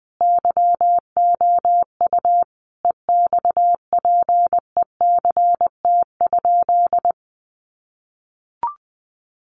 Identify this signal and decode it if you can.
Morse code, heard as a keyed tone, about 20 wpm, 700 Hz. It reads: YOU EXPECT?
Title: Sentences from Top 1100 Words 20wpm